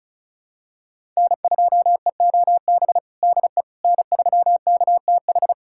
Morse code, heard as a keyed tone, about 34 wpm, 700 Hz. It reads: N2EOB DE N3KTH
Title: USA Callsign DE Callsign - 5 Characters - 35wpm